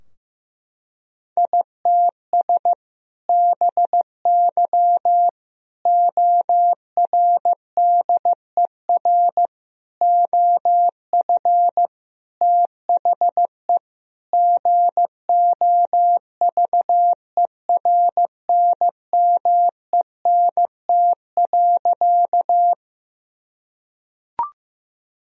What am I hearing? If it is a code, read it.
Morse code, heard as a keyed tone, about 15 wpm, 700 Hz. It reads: ITS BY ORDER OF THE GOVERNMENT.
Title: Sentences from Top 1000 Words - Review Part 3 - 15wpm